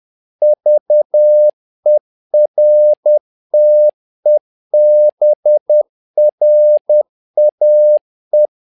Morse code, heard as a keyed tone, about 10 wpm, 600 Hz. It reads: VERTEBRAE
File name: learning center / tower trouble / tower trouble 10wpm